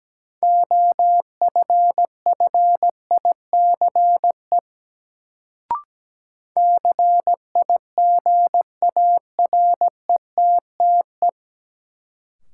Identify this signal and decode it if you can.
Morse code, heard as a keyed tone, about 17 wpm, 700 Hz. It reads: OFFICE CIGARETTE
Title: Sentences from Top 1000 Words - New Word Drill - 17wpm